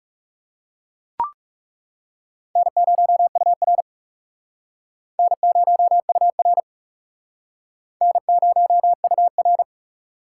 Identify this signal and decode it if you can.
Morse code, heard as a keyed tone, about 38 wpm, 700 Hz. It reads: N0UR N0UR N0UR